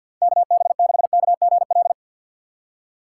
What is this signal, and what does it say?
Morse code, heard as a keyed tone, about 53 wpm, 700 Hz. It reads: KB6KGL
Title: USA Call Signs - 3x Character Spacing - 50wpm